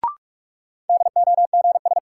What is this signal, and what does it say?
Morse code, heard as a keyed tone, about 43 wpm, 700 Hz. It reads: DOGS
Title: Sentences from Top 1600 Words - New Word Drill - 45wpm